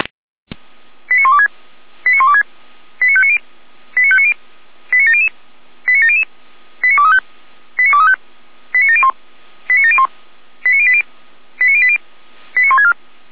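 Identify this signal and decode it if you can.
Five-tone selective call (ZVEI), heard as quick runs of five tones, tones of about 70 ms. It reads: 89126 89126 89590 89590 89700 89700 89235 89235 89701 89701 89080 89080 89164
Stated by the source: Received by SDR, ZVEI five-tone selective call 89340, fire brigade Feuerwehr Orlamünde.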